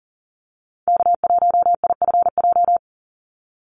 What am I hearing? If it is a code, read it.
Morse code, heard as a keyed tone, about 41 wpm, 700 Hz. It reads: K1IFJ
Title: Call Signs - 40wpm